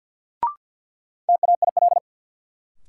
Morse code, heard as a keyed tone, about 53 wpm, 700 Hz. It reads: TAIL
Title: Sentences from Top 1000 Words - Review Part 3 - New Word Drill - 50wpm